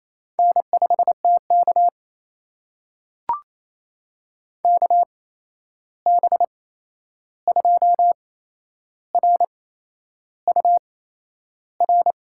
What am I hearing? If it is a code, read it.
Morse code, heard as a keyed tone, about 28 wpm, spaced as for 11 wpm, 700 Hz. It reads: N5TK KB2RUR
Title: USA Call Signs - 8x Character Spacing - 28wpm